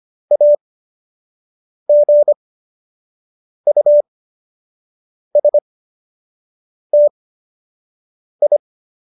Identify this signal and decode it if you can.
Morse code, heard as a keyed tone, about 25 wpm, spaced as for 6 wpm, 590 Hz. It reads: AGUSTI